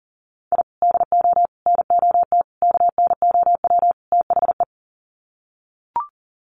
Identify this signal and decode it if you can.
Morse code, heard as a keyed tone, about 40 wpm, 700 Hz. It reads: I DO NOT KNOW THE
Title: Sets of 5 Words 40wpm